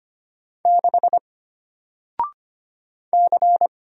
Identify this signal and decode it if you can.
Morse code, heard as a keyed tone, about 25 wpm, 700 Hz. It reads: 6 C
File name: Single-Letter-Number-25wpm